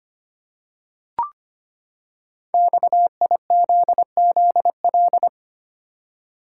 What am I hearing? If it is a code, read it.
Morse code, heard as a keyed tone, about 25 wpm, 700 Hz. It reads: XIZZL
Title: Code Groups - 5 Letters - 25wpm